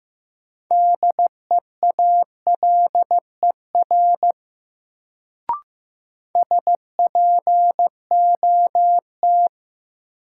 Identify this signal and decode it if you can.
Morse code, heard as a keyed tone, about 15 wpm, 700 Hz. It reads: DEALER SPOT
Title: Sentences from Top 1100 Words - New Word Drill Alt - 15wpm